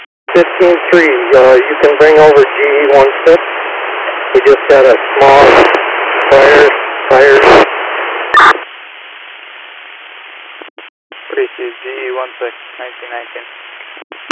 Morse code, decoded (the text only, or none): none